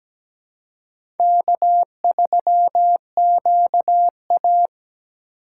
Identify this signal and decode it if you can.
Morse code, heard as a keyed tone, about 17 wpm, 700 Hz. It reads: K3QA
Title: USA Call Signs - 4 Characters - 3x Character Spacing - 17wpm